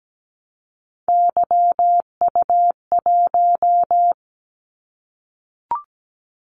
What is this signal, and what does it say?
Morse code, heard as a keyed tone, about 17 wpm, 700 Hz. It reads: YU1
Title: Callsign Prefixes for Top 25 DXCC - 17wpm